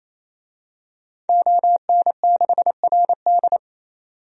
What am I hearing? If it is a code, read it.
Morse code, heard as a keyed tone, about 28 wpm, 700 Hz. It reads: ON6RD
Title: DXCC-100 Call Signs - 5 Characters - 28wpm